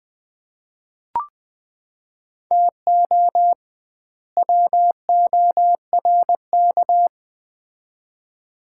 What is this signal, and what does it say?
Morse code, heard as a keyed tone, about 20 wpm, 700 Hz. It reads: TO WORK
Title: 2x Word-Spacing - Sets of 2 Words - 20wpm